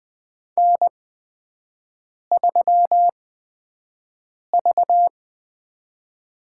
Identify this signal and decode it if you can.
Morse code, heard as a keyed tone, about 20 wpm, spaced as for 5 wpm, 700 Hz. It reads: N3V
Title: USA Call Signs - 4 Characters - 8x Character Spacing - 20wpm